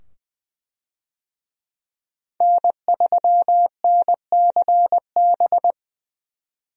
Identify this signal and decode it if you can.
Morse code, heard as a keyed tone, about 20 wpm, 700 Hz. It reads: N3NCB
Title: USA Call Signs - 5 Characters - 20wpm